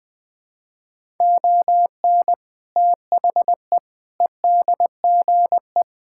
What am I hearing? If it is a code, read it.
Morse code, heard as a keyed tone, about 20 wpm, 700 Hz. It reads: ON THE EDGE
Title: Sentences with 3-Word Phrases - 20wpm